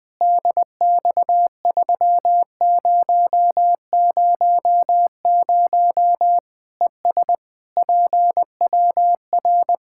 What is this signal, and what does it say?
Morse code, heard as a keyed tone, about 20 wpm, 700 Hz. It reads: DX3000 ES PWR
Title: Practice QSO - Exchange 2 - Info - 20wpm